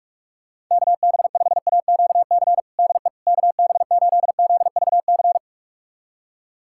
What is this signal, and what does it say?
Morse code, heard as a keyed tone, about 46 wpm, 700 Hz. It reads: KB5AQC DE KB8ZVC
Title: USA Callsign DE Callsign - 6 Characters - 45wpm